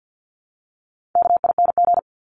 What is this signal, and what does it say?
Morse code, heard as a keyed tone, about 50 wpm, 700 Hz. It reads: KING
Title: Top 500 Words 50wpm